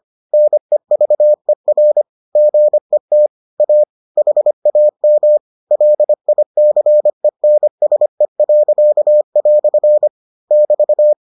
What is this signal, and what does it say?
Morse code, heard as a keyed tone, about 25 wpm, 600 Hz. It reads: NEVER GET A HAM LICENSE." <BT>